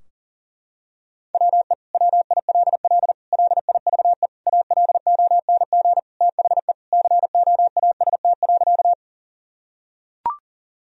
Morse code, heard as a keyed tone, about 40 wpm, 700 Hz. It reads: WE WILL LIVE ALONG THE COAST.
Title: Sentences from Top 1000 Words - Review Part 4 - 40wpm